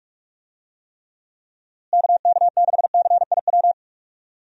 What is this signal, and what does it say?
Morse code, heard as a keyed tone, about 46 wpm, 700 Hz. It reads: KK6CIW
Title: USA Call Signs - 6 Characters - Repeated 2x - 45wpm